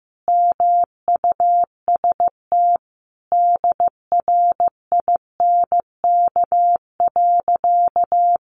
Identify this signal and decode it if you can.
Morse code, heard as a keyed tone, about 15 wpm, 700 Hz. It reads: MUST DRINK.
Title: Sentences from Top 1500 Words 15wpm